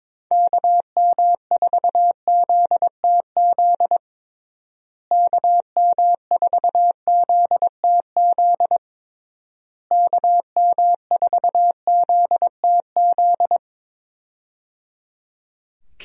Morse code, heard as a keyed tone, about 22 wpm, 700 Hz. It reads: KM4ZTZ KM4ZTZ KM4ZTZ
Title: USA Call Signs - 6 Characters - Repeated 3x - 22wpm